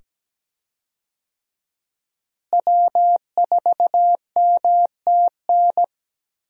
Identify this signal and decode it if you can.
Morse code, heard as a keyed tone, about 17 wpm, 700 Hz. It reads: W4MTN
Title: USA Call Signs - 5 Characters - Repeated 2x - 17wpm